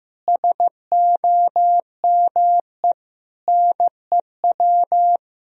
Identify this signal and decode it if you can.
Morse code, heard as a keyed tone, about 15 wpm, 700 Hz. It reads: SOME NEW
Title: Sentences from Top 1000 Words - Review Part 4 - 15wpm